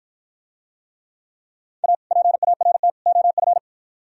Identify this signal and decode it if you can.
Morse code, heard as a keyed tone, about 50 wpm, 700 Hz. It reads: A PART OF